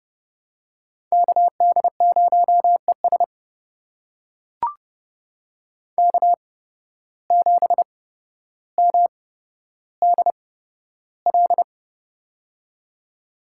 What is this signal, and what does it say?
Morse code, heard as a keyed tone, about 30 wpm, spaced as for 13 wpm, 700 Hz. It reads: KD0ES K7MDL
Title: USA Call Signs - 8x Character Spacing - 30wpm